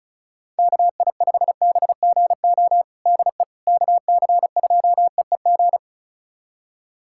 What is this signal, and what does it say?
Morse code, heard as a keyed tone, about 36 wpm, 700 Hz. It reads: KI5BGO DE KC2EEG
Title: USA Callsign DE Callsign 35wpm